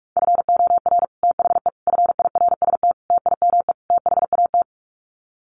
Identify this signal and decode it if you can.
Morse code, heard as a keyed tone, about 44 wpm, 700 Hz. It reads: FOR THE FIRST TIME THAT
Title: Sets of 5 Words Encore 45wpm